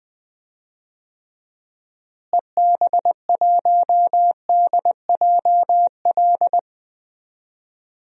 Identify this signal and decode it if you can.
Morse code, heard as a keyed tone, about 20 wpm, 700 Hz. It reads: EB1DJL
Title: DXCC-100 Call Signs - 6 Characters - Repeated 2x - 20wpm